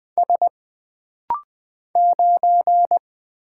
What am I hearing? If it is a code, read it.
Morse code, heard as a keyed tone, about 20 wpm, 700 Hz. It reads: S 9
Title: Single Letter-Number - Rapid-Fire 20wpm